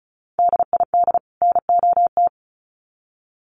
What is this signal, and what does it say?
Morse code, heard as a keyed tone, about 34 wpm, 700 Hz. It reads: DID NOT
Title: Sets of 2 Words 35wpm